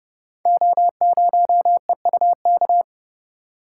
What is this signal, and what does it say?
Morse code, heard as a keyed tone, about 30 wpm, 700 Hz. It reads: O0EUK
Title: Code Groups - 5 Letter:Numbers - 30wpm